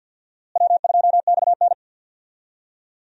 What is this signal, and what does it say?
Morse code, heard as a keyed tone, about 51 wpm, 700 Hz. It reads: W2XN
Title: USA Call Signs - 4 Characters - Repeated 2x - 50wpm